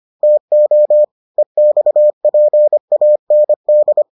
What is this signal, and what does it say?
Morse code, heard as a keyed tone, about 25 wpm, 600 Hz. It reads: TO EXPAND